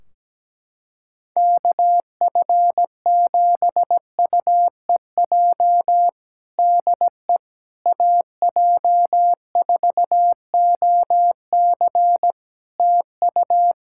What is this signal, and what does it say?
Morse code, heard as a keyed tone, about 17 wpm, 700 Hz. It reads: KF7UEJ DE AJ4OC TU
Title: Practice QSO - Exchange 4 - Ending - 17wpm